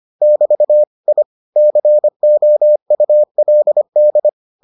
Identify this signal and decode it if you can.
Morse code, heard as a keyed tone, about 25 wpm, 600 Hz. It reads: <BT> I COULD